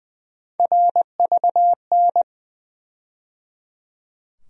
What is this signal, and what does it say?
Morse code, heard as a keyed tone, about 21 wpm, 700 Hz. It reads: RVN
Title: USA Call Sign Postfixes - 3 Letters 20wpm